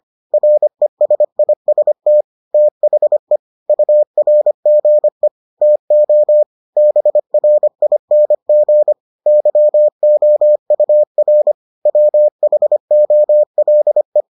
Morse code, heard as a keyed tone, about 25 wpm, 600 Hz. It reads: RESIST THE URGE TO BRING YOUR WHOLE